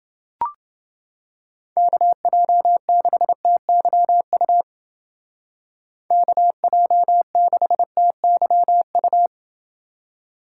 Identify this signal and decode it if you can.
Morse code, heard as a keyed tone, about 28 wpm, 700 Hz. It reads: KJ6TYU KJ6TYU